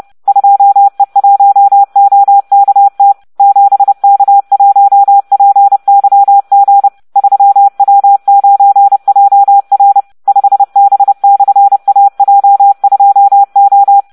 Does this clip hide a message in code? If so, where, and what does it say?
Yes: Morse code from 0.3 to 14.0 s, heard as a keyed tone, about 30 wpm, 800 Hz. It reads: 2E1OKT 7K1PYG 3W9JR 5B/AJ2O